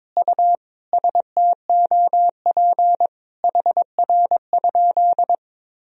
Morse code, heard as a keyed tone, about 22 wpm, 700 Hz. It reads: U STOP HR?
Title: Sentences from Top 300 Words and Abbreviations 22wpm